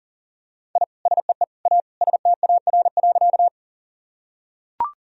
Morse code, heard as a keyed tone, about 40 wpm, 700 Hz. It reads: I SEE A STAR.
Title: Sentences from Top 1000 Words - Review Part 2 - 40wpm